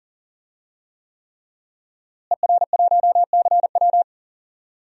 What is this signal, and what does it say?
Morse code, heard as a keyed tone, about 40 wpm, 700 Hz. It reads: ER1CW
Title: DXCC-100 Call Signs - 5 Characters - Repeated 3x - 40wpm